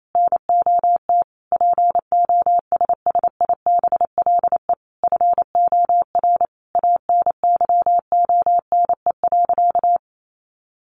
Morse code, heard as a keyed tone, about 28 wpm, 700 Hz. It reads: NOT POSSIBLE FOR ANYONE.
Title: Sentences from Top 900 Words 28wpm